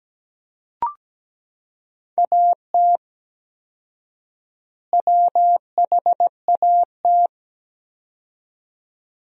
Morse code, heard as a keyed tone, about 17 wpm, 700 Hz. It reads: AT WHAT
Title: 4x Word-Spacing - Sentences from Top 100 Words - 17wpm